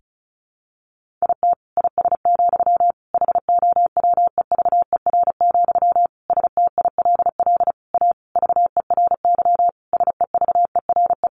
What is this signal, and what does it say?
Morse code, heard as a keyed tone, about 34 wpm, 700 Hz. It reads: IT IS, HOWEVER, STILL A VERY SEVERE
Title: Sentences from Top 1400 Words 35wpm